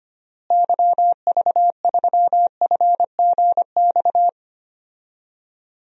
Morse code, heard as a keyed tone, about 25 wpm, 700 Hz. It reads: YV3FGX
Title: DXCC-100 Call Signs - 6 Characters - Repeated 2x - 25wpm